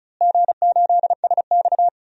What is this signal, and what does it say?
Morse code, heard as a keyed tone, about 36 wpm, 700 Hz. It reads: G8SX
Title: DXCC-100 Call Signs - 4 Characters - Repeated 2x - 35wpm